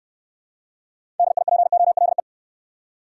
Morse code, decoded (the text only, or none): DIPOLE